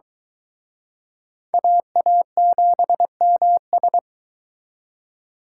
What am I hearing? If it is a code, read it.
Morse code, heard as a keyed tone, about 22 wpm, 700 Hz. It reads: AA7MS